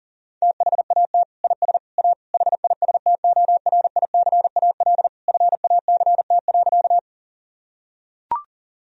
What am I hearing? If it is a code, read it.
Morse code, heard as a keyed tone, about 40 wpm, 700 Hz. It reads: THAT IS A HISTORICAL FACT.
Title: Sentences from Top 1000 Words - Review Part 4 - 40wpm